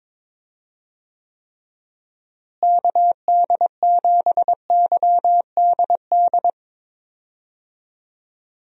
Morse code, heard as a keyed tone, about 22 wpm, 700 Hz. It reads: KD7YDD
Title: USA Call Signs - 6 Characters - Repeated 2x - 22wpm